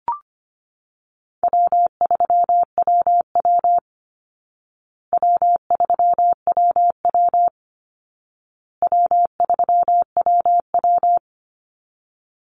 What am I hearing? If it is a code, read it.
Morse code, heard as a keyed tone, about 25 wpm, 700 Hz. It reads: W3WW W3WW W3WW
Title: Call Signs Repeated 2x - 25wpm